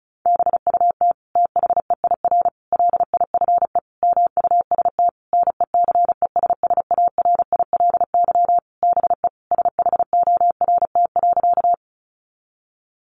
Morse code, heard as a keyed tone, about 34 wpm, 700 Hz. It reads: BUT THEIR LIFE MUST NECESSARILY BE SHORT.
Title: Sentences from Top 1400 Words 35wpm